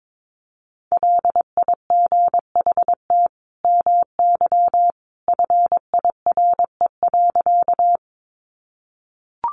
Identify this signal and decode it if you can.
Morse code, heard as a keyed tone, about 22 wpm, 700 Hz. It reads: LIGHT MY FIRE.
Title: Sentences from Top 400 Words 22wpm